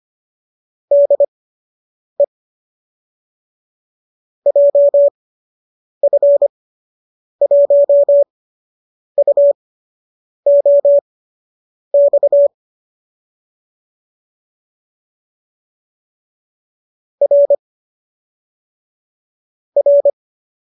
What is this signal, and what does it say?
Morse code, heard as a keyed tone, about 25 wpm, spaced as for 8 wpm, 570 Hz. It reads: DE JF1UOX R R